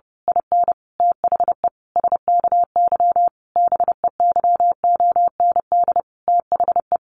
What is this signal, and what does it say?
Morse code, heard as a keyed tone, about 30 wpm, 700 Hz. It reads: IN THE SKY BEYOND THE
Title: A Sound Of Thunder 30wpm